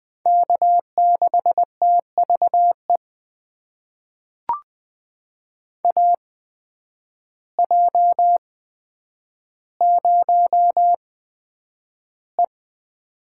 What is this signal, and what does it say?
Morse code, heard as a keyed tone, about 20 wpm, spaced as for 9 wpm, 700 Hz. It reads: K6TVE AJ0E
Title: USA Call Signs - 8x Character Spacing - 20wpm